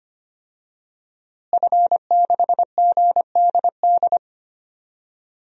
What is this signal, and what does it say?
Morse code, heard as a keyed tone, about 25 wpm, 700 Hz. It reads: F6GDD
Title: DXCC-100 Call Signs - 5 Characters - Repeated 3x - 25wpm